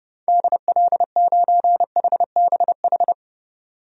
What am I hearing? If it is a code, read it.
Morse code, heard as a keyed tone, about 32 wpm, 700 Hz. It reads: DL9HBH
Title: DXCC-100 Call Signs - 6 Characters - 30wpm